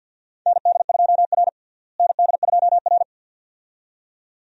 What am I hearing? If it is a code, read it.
Morse code, heard as a keyed tone, about 52 wpm, 700 Hz. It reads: ND2R ND2R
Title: USA Call Signs - 4 Characters - Repeated 2x - 50wpm